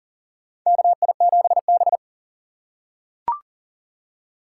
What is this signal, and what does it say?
Morse code, heard as a keyed tone, about 40 wpm, 700 Hz. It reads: KI7B
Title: USA Call Signs - 4 Characters - 6x Character Spacing - 40wpm